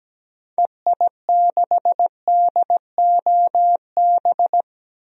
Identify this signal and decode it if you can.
Morse code, heard as a keyed tone, about 17 wpm, 700 Hz. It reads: EI6DOB
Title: DXCC-100 Call Signs - 6 Characters - Repeated 3x - 17wpm